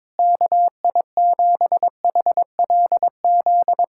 Morse code, heard as a keyed tone, about 22 wpm, 700 Hz. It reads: KI7HLZ
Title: USA Callsign DE Callsign - 6 Characters - 22wpm